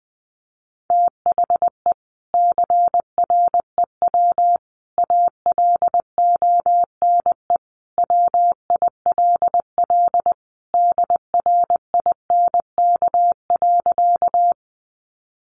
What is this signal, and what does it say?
Morse code, heard as a keyed tone, about 20 wpm, 700 Hz. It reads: THE CREW ALONE WILL DRINK.
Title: Sentences from Top 1500 Words 20wpm